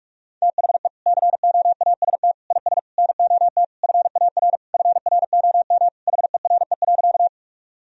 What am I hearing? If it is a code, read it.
Morse code, heard as a keyed tone, about 42 wpm, 700 Hz. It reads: THE COAST IS NOT FAR FROM HERE.
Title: Sentences from Top 1000 Words - Review Part 4 - 45wpm